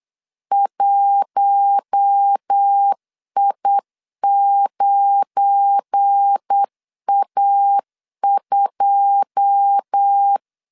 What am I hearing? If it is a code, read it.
Morse code, heard as a keyed tone, about 8 wpm, 800 Hz. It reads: 1I9A2
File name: LEZIONI / 9 2 8wpm 3437